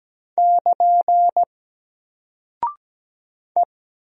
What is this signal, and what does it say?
Morse code, heard as a keyed tone, about 17 wpm, 700 Hz. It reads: <KN> E
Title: Single Character (Letter:Number:Punctuation:Prosign) 17wpm